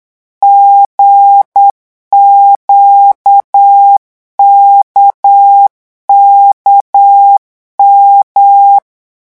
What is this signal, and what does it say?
Morse code, heard as a keyed tone, about 8 wpm, 800 Hz. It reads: GQKKM
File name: morse mp3 / revision3-vitesse 8 mots